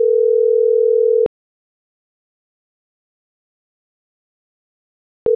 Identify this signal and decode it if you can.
Ringback tone (North America), 440 + 480 Hz.